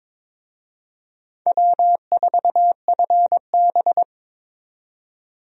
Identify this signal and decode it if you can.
Morse code, heard as a keyed tone, about 22 wpm, 700 Hz. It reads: W4FB
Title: USA Call Signs - 4 Characters - 22wpm